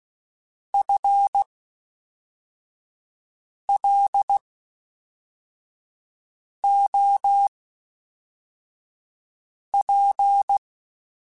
Morse code, heard as a keyed tone, about 16 wpm, spaced as for 4 wpm, 800 Hz. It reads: FLOP
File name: Spaced / FL Runs sp mp3s / FL Runs sp @5